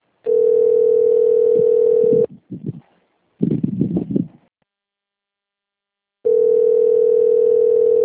Ringback tone (North America), 440 + 480 Hz.
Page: Sao Roque